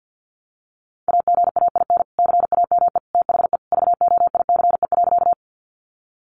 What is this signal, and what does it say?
Morse code, heard as a keyed tone, about 50 wpm, 700 Hz. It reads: AGAIN CAME THE VOICE.
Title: Sentences from Top 500 Words 50wpm